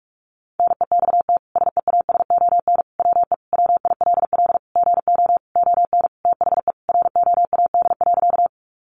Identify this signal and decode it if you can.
Morse code, heard as a keyed tone, about 44 wpm, 700 Hz. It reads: NEXT SEASON WE WILL GO ON THE ROAD.
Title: Sentences from Top 1400 Words 45wpm